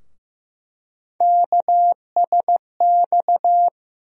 Morse code, heard as a keyed tone, about 15 wpm, 700 Hz. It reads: KSX
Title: USA Call Sign Postfixes - 3 Letters 15wpm